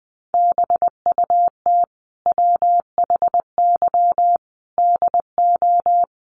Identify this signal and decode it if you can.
Morse code, heard as a keyed tone, about 20 wpm, 700 Hz. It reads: BUT WHY DO
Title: Sentences from Top 200 Words 20wpm